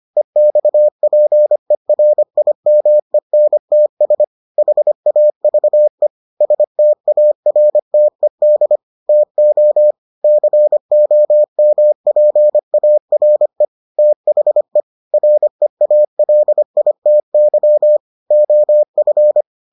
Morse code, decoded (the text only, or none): EXPERIMENTS HAVE STARTED TO COMPARE THE REALITY OF